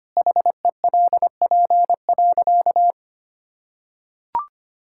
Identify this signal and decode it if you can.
Morse code, heard as a keyed tone, about 25 wpm, 700 Hz. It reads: HELP.
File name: practice / 25wpm / Sentences with 4-Word Phrases - 25wpm